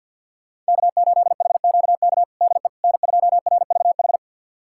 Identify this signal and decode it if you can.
Morse code, heard as a keyed tone, about 51 wpm, 700 Hz. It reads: K8HQK DE N2RVH
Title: USA Callsign DE Callsign - 5 Characters - 50wpm